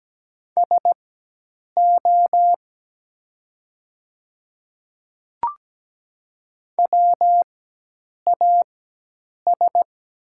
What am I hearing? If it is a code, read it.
Morse code, heard as a keyed tone, about 17 wpm, spaced as for 8 wpm, 700 Hz. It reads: SO WAS
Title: 4x Character Spacing - Top 100 Words - 17wpm